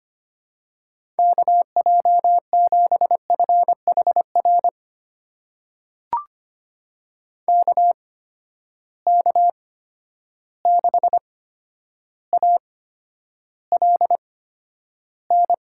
Morse code, heard as a keyed tone, about 25 wpm, spaced as for 11 wpm, 700 Hz. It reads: KJ7FHR KK6ALN
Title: USA Call Signs - 6 Characters - 8x Character Spacing - 25wpm